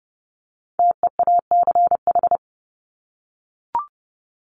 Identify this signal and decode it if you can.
Morse code, heard as a keyed tone, about 30 wpm, 700 Hz. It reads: TEACH
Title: Top 500 Words Repeated 30wpm